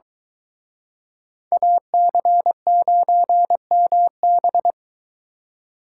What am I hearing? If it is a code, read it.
Morse code, heard as a keyed tone, about 23 wpm, 700 Hz. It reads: AC9MB